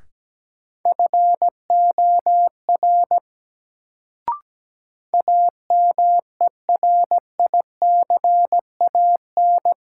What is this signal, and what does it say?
Morse code, heard as a keyed tone, about 17 wpm, 700 Hz. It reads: FOR AMERICAN
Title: Sets of 2 Words - New Word Drill - 17wpm